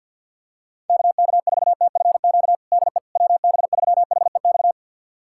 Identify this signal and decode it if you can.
Morse code, heard as a keyed tone, about 52 wpm, 700 Hz. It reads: KK4TFQ DE WB3HEX
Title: USA Callsign DE Callsign - 6 Characters - 50wpm